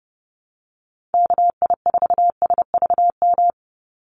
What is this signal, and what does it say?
Morse code, heard as a keyed tone, about 28 wpm, 700 Hz. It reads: KI4SVM
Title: NA SOTA Activator Callsigns 30wpm